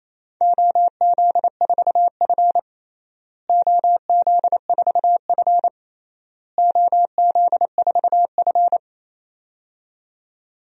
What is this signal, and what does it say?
Morse code, heard as a keyed tone, about 28 wpm, 700 Hz. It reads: OZ4F OZ4F OZ4F
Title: DXCC-100 Call Signs - 4 Characters - Repeated 3x - 28wpm